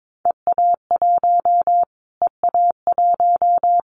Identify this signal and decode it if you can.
Morse code, heard as a keyed tone, about 22 wpm, 700 Hz. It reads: EA1 EA1
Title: Callsign Prefixes for Top 25 DXCC - Repeated 3x - 22wpm